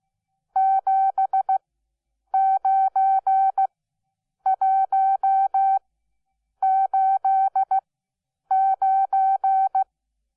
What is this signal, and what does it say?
Morse code, heard as a keyed tone, about 16 wpm, spaced as for 8 wpm, 770 Hz. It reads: 79189
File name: Onlinelehrgang / morsen / DJ4UF Morsen12